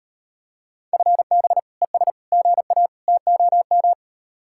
Morse code, heard as a keyed tone, about 38 wpm, 700 Hz. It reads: FB ES GA TOM